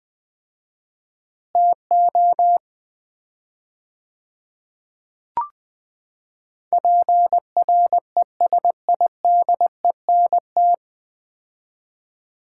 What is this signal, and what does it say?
Morse code, heard as a keyed tone, about 20 wpm, 700 Hz. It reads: TO PRESIDENT
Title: 4x Word-Spacing - Sets of 3 Words - 20wpm